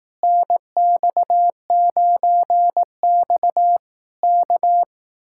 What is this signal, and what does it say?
Morse code, heard as a keyed tone, about 18 wpm, 700 Hz. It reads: NX9X K
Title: Speed-Racing Common QSO Phrases Encore - 15wpm